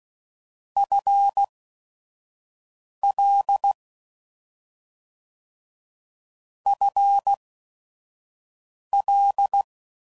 Morse code, heard as a keyed tone, about 16 wpm, spaced as for 5 wpm, 800 Hz. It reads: FL FL